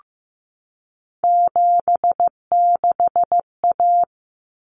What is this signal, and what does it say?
Morse code, heard as a keyed tone, about 15 wpm, 700 Hz. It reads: 76A
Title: Common Cut Numbers - 3-Digit 15wpm